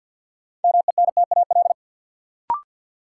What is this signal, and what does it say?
Morse code, heard as a keyed tone, about 46 wpm, 700 Hz. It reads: MENTAL
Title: Sentences from Top 1000 Words - Review Part 4 - New Word Drill - 50wpm